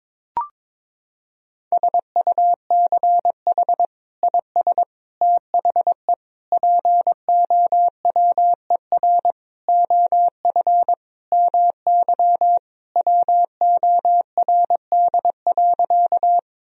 Morse code, heard as a keyed tone, about 22 wpm, 700 Hz. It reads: SUCH IS THE POWER OF MY WORD.
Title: Sentences from Top 1000 Words - Review Part 2 - 22wpm